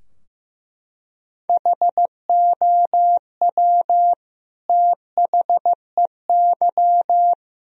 Morse code, heard as a keyed tone, about 16 wpm, 700 Hz. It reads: HOW THEY
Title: Sentences from Top 1000 Words - Review Part 4 - 15wpm